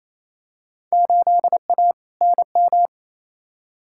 Morse code, heard as a keyed tone, about 27 wpm, 700 Hz. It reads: 8A NM
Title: Field Day Exchanges Encore - 28wpm